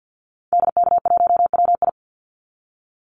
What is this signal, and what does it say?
Morse code, heard as a keyed tone, about 50 wpm, 700 Hz. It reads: DK1WI
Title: Call Signs - 50wpm